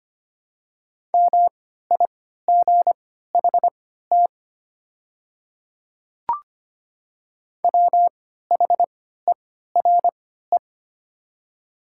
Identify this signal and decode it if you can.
Morse code, heard as a keyed tone, about 25 wpm, spaced as for 14 wpm, 700 Hz. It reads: MIGHT WHERE
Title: 3x Character Spacing - Top 300 Words - 25wpm